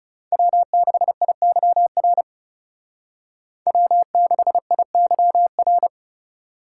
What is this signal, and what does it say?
Morse code, heard as a keyed tone, about 39 wpm, 700 Hz. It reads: W6IYR W6IYR